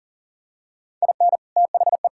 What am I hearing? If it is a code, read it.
Morse code, heard as a keyed tone, about 40 wpm, 700 Hz. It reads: IN THE
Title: Sentences from Top 1000 Words - Review Part 5 - 40wpm